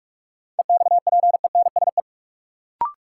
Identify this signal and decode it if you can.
Morse code, heard as a keyed tone, about 44 wpm, 700 Hz. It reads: EXPENSE
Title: Sets of 4 Words - New Word Drill - 45wpm